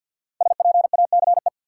Morse code, heard as a keyed tone, about 52 wpm, 700 Hz. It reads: SPACE
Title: Sentences from Top 1500 Words - New Word Drill Alt - 50wpm